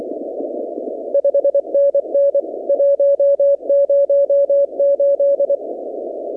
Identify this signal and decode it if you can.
Morse code, heard as a keyed tone, about 23 wpm, 560 Hz. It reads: 5NN 108
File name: contests / 2019-cw-wpx-kt5j-k5pi- -2330 0-38